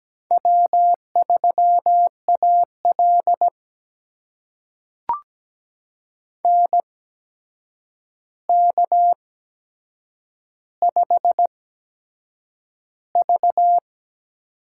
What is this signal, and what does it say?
Morse code, heard as a keyed tone, about 17 wpm, spaced as for 7 wpm, 700 Hz. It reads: W3AL NK5V
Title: USA Call Signs - 4 Characters - 8x Character Spacing - 17wpm